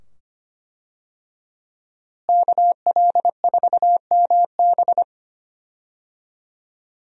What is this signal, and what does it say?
Morse code, heard as a keyed tone, about 25 wpm, 700 Hz. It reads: KL4MB
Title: USA Call Signs - 5 Characters - Repeated 2x - 25wpm